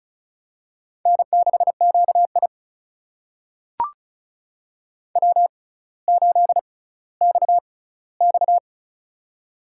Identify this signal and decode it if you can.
Morse code, heard as a keyed tone, about 36 wpm, spaced as for 18 wpm, 700 Hz. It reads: N6QI W8XX
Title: USA Call Signs - 4 Characters - 6x Character Spacing - 35wpm